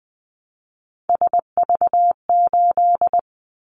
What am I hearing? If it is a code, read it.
Morse code, heard as a keyed tone, about 20 wpm, 700 Hz. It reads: SV8
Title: Callsign Prefixes for Top 50 DXCC - 20wpm